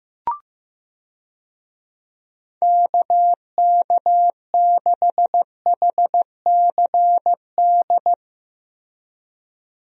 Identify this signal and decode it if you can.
Morse code, heard as a keyed tone, about 15 wpm, 700 Hz. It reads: KK6HCD
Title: USA Call Signs - 6 Characters - Repeated 3x - 15wpm